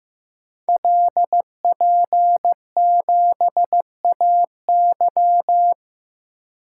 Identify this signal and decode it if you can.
Morse code, heard as a keyed tone, about 15 wpm, 700 Hz. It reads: LP7AY
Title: Code Groups - 5 Letter:Numbers - 15wpm